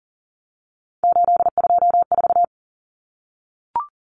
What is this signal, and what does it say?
Morse code, heard as a keyed tone, about 40 wpm, 700 Hz. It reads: 824
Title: Numbers - 3-Digit 40wpm